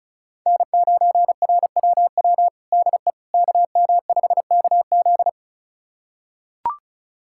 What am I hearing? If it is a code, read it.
Morse code, heard as a keyed tone, about 34 wpm, 700 Hz. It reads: N9RWW DE KM5KZ
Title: USA Callsign DE Callsign - 5 Characters - 35wpm